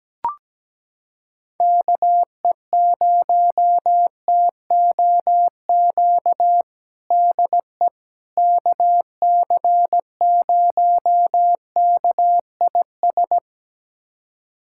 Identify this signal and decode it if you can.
Morse code, heard as a keyed tone, about 17 wpm, 700 Hz. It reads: KE0TOQ DE KC0KIS
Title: USA Callsign DE Callsign - 6 Characters - 17wpm